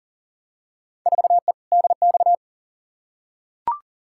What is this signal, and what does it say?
Morse code, heard as a keyed tone, about 40 wpm, 700 Hz. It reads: 4E DX
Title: Field Day Exchanges Encore - 40wpm